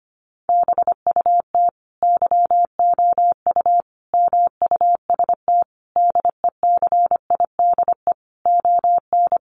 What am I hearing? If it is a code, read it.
Morse code, heard as a keyed tone, about 25 wpm, 700 Hz. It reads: BUT YOU MUST DECIDE ON
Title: Sentences from Top 600 Words 25wpm